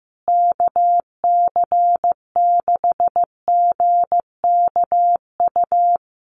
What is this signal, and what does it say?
Morse code, heard as a keyed tone, about 15 wpm, 700 Hz. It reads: KC6GKU
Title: USA Call Signs Repeated 3x 15wpm